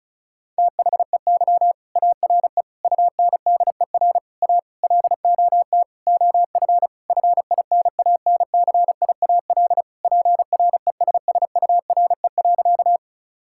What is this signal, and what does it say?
Morse code, heard as a keyed tone, about 36 wpm, 700 Hz. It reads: THEY ARE UNDER A LOT OF FINANCIAL PRESSURE.
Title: Sentences from Top 1000 Words - Review Part 6 - 35wpm